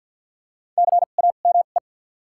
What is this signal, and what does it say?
Morse code, heard as a keyed tone, about 52 wpm, 700 Hz. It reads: C A M E
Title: 2x Character Spacing - Top 200 Words - 50wpm